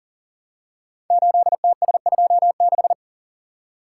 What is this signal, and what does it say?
Morse code, heard as a keyed tone, about 40 wpm, 700 Hz. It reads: 8TS26
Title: Code Groups - 5 Letter:Numbers - 40wpm